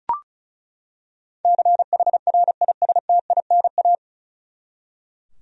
Morse code, heard as a keyed tone, about 36 wpm, 700 Hz. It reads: CHRISTINA
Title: USA Common Names - 35wpm